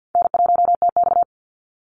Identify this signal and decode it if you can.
Morse code, heard as a keyed tone, about 50 wpm, 700 Hz. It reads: N1TX
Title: Easy Call Signs - 50wpm